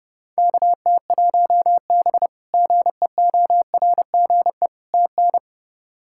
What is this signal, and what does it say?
Morse code, heard as a keyed tone, about 31 wpm, 700 Hz. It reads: KT1B GEORGE TN
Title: K1USN SST QSO Practice - 30wpm